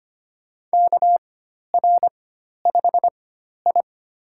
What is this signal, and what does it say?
Morse code, heard as a keyed tone, about 25 wpm, spaced as for 12 wpm, 700 Hz. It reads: KR5I
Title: USA Call Signs - 4 Characters - 4x Character Spacing - 25wpm